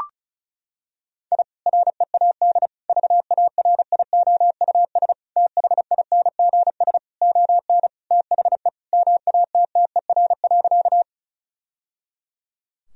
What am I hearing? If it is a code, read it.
Morse code, heard as a keyed tone, about 36 wpm, 700 Hz. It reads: I READ VARIOUS THINGS ON THE MATTER.
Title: Sentences from Top 1000 Words - Review Part 4 - 35wpm